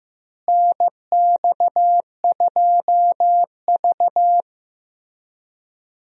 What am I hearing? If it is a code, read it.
Morse code, heard as a keyed tone, about 15 wpm, 700 Hz. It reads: NX2V
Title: USA Call Signs - 4 Characters - Repeated 3x - 15wpm